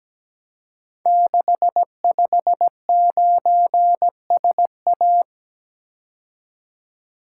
Morse code, heard as a keyed tone, about 17 wpm, 700 Hz. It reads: 659SA
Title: Code Groups - 5 Letter:Numbers - 17wpm